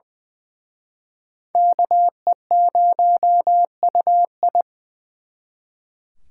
Morse code, heard as a keyed tone, about 24 wpm, 700 Hz. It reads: KE0UI
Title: Speed Racing - USA Call Signs 20wpm